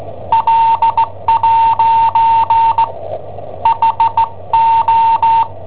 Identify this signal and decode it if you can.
Morse code, heard as a keyed tone, about 16 wpm, spaced as for 12 wpm, 910 Hz. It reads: L'HO
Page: Cassetta esercizi preparata da I7OHP per IW5ELP